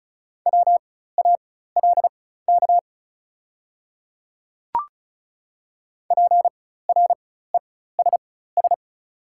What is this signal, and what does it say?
Morse code, heard as a keyed tone, about 36 wpm, spaced as for 16 wpm, 700 Hz. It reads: WALK PRESS
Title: 4x Character Spacing - Top 300 Words - 35wpm